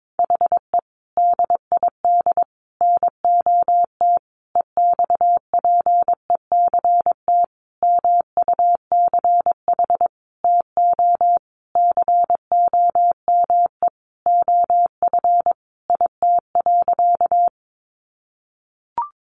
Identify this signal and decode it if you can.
Morse code, heard as a keyed tone, about 22 wpm, 700 Hz. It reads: HE DID NOT EXPECT MUCH TO COME OF IT.
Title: Sentences from Top 1100 Words 22wpm